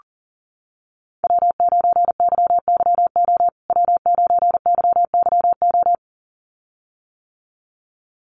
Morse code, heard as a keyed tone, about 40 wpm, 700 Hz. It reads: W9YYO W9YYO
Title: USA Call Signs Repeated 2x 40wpm